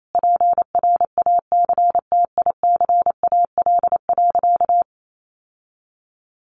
Morse code, heard as a keyed tone, about 28 wpm, 700 Hz. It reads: PRACTICAL.
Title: Sentences from Top 1400 Words 28wpm